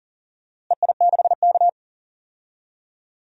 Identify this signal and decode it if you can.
Morse code, heard as a keyed tone, about 40 wpm, 700 Hz. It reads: EI6K
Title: DXCC-100 Call Signs - 4 Characters - Repeated 2x - 40wpm